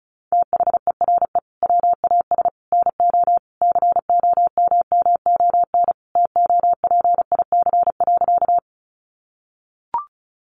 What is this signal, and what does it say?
Morse code, heard as a keyed tone, about 34 wpm, 700 Hz. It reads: THERE WAS NO COMMON TOPIC.
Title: Sentences from Top 1200 Words 35wpm